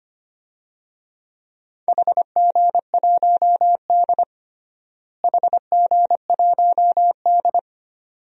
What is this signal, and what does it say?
Morse code, heard as a keyed tone, about 25 wpm, 700 Hz. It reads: HG1D HG1D
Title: DXCC-100 Call Signs - 4 Characters - Repeated 2x - 25wpm